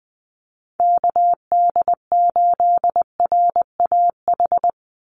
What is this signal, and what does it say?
Morse code, heard as a keyed tone, about 19 wpm, 700 Hz. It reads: KD8RAH
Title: USA Call Signs 20wpm